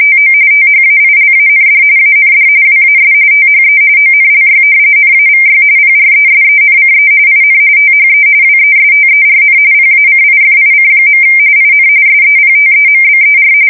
RTTY, about 45.45 baud. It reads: TO THAILAND AND WAS THE ONLY NAVY PRESENCE IN THE WESTERN PACIFIC. IN SEPTEMBER
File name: fsk170 / battleshipnj